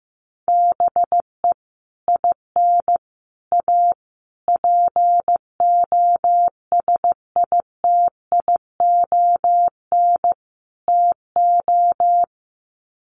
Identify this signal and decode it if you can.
Morse code, heard as a keyed tone, about 15 wpm, 700 Hz. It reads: BE IN A POSITION TO
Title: Sets of 5 Words Encore2 15wpm